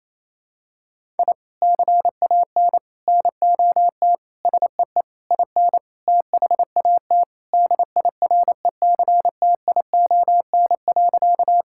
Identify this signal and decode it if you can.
Morse code, heard as a keyed tone, about 28 wpm, 700 Hz. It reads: I CAN NOT SEE IN THAT DIRECTION.
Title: Sentences from Top 1000 Words - Review Part 5 - 28wpm